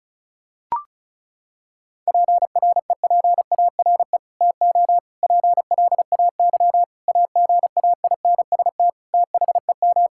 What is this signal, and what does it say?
Morse code, heard as a keyed tone, about 36 wpm, 700 Hz. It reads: PREPARE TO PLAY AGAINST THEM
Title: Sentences from Top 1000 Words - Review Part 6 - 35wpm